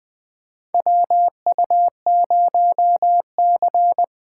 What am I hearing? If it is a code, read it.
Morse code, heard as a keyed tone, about 20 wpm, 700 Hz. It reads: WU0C
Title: USA Call Signs - 8x Character Spacing - 20wpm